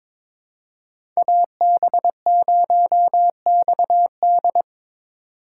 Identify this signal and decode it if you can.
Morse code, heard as a keyed tone, about 22 wpm, 700 Hz. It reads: AB0XD
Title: USA Call Signs - 5 Characters - 4x Character Spacing - 22wpm